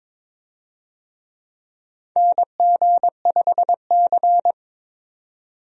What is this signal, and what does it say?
Morse code, heard as a keyed tone, about 22 wpm, 700 Hz. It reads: NG5C
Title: USA Call Signs - 4 Characters - 22wpm